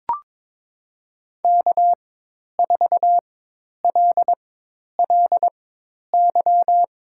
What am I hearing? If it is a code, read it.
Morse code, heard as a keyed tone, about 22 wpm, spaced as for 10 wpm, 700 Hz. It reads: K4LLY
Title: USA Call Signs - 5 Characters - 4x Character Spacing - 22wpm